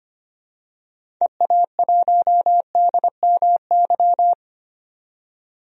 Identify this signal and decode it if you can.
Morse code, heard as a keyed tone, about 25 wpm, 700 Hz. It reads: EA1DMY
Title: DXCC-100 Call Signs - 6 Characters - 25wpm